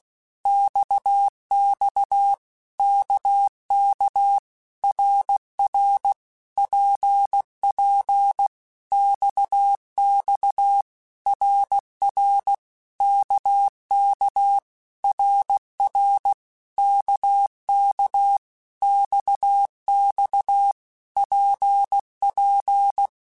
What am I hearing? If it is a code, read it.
Morse code, heard as a keyed tone, about 16 wpm, 800 Hz. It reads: XX KK RR PP XX RR KK RR KK XX PP
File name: Basic Letter Groups By7Groups / RKPX / RKPX@15